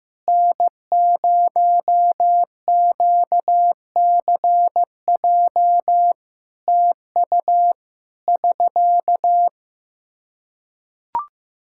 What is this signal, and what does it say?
Morse code, heard as a keyed tone, about 15 wpm, 700 Hz. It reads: N0QCJ TU <SK>
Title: Practice QSO - Exchange 4 - Ending - 15wpm